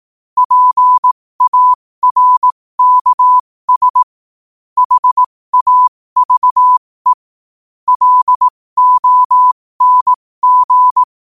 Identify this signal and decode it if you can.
Morse code, heard as a keyed tone, about 18 wpm, spaced as for 15 wpm, 1000 Hz. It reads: PARKS HAVE LONG
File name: qotdfiles / 15wpm